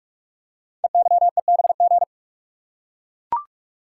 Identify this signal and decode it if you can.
Morse code, heard as a keyed tone, about 44 wpm, 700 Hz. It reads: EYEBG
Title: Code Groups - 5 Letters - 45wpm